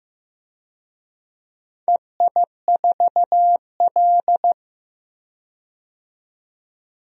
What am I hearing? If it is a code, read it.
Morse code, heard as a keyed tone, about 15 wpm, 700 Hz. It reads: EI4L
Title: DXCC-100 Call Signs - 4 Characters - 15wpm